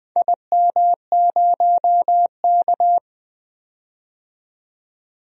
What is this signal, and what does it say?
Morse code, heard as a keyed tone, about 20 wpm, 700 Hz. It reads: IM0K
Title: DXCC-100 Call Signs - 4 Characters - 20wpm